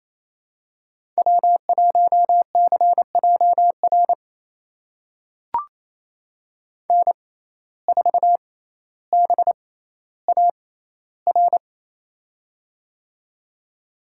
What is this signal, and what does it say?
Morse code, heard as a keyed tone, about 28 wpm, spaced as for 14 wpm, 700 Hz. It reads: W1CJR N4BAR
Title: USA Call Signs - 6x Character Spacing - 28wpm